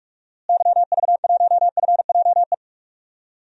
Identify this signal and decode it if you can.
Morse code, heard as a keyed tone, about 46 wpm, 700 Hz. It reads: YV1FJE
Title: DXCC-100 Call Signs - 6 Characters - Repeated 3x - 45wpm